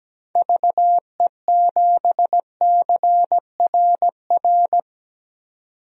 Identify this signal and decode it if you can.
Morse code, heard as a keyed tone, about 17 wpm, 700 Hz. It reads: VE7CRR
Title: DXCC-100 Call Signs - 6 Characters - 17wpm